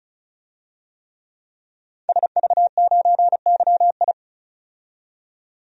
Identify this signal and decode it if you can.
Morse code, heard as a keyed tone, about 36 wpm, 700 Hz. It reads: SV9YI
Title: DXCC-100 Call Signs - 5 Characters - 35wpm